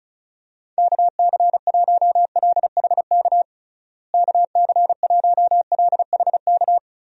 Morse code, heard as a keyed tone, about 34 wpm, 700 Hz. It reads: KC1LHK KC1LHK
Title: USA Call Signs - 6 Characters - Repeated 3x - 35wpm